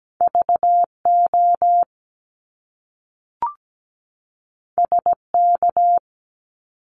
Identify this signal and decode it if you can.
Morse code, heard as a keyed tone, about 17 wpm, 700 Hz. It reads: VO SK
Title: Top Two Letter Combinations in Words 17wpm